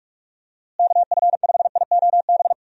Morse code, heard as a keyed tone, about 46 wpm, 700 Hz. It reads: KF5IOB
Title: USA Call Signs - 6 Characters - Repeated 2x - 45wpm